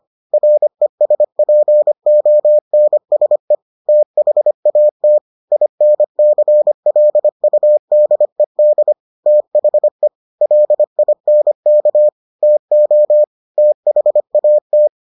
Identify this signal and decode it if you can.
Morse code, heard as a keyed tone, about 25 wpm, 600 Hz. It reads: RESPONSE THAT INCLUDED THE LINK TO THAT